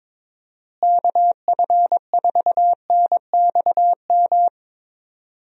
Morse code, heard as a keyed tone, about 22 wpm, 700 Hz. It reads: KF4NXM
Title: USA Call Signs - 6 Characters - Repeated 2x - 22wpm